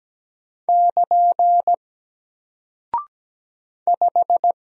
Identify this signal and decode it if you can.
Morse code, heard as a keyed tone, about 17 wpm, 700 Hz. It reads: <KN> 5
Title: Single Character (Letter:Number:Punctuation:Prosign) Alt 17wpm